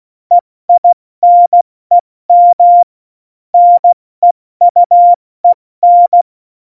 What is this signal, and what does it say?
Morse code, heard as a keyed tone, about 16 wpm, 700 Hz. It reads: EINEM NEUEN
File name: Morsekurs / Witze / Witz 15. Januar 2019, 70 F80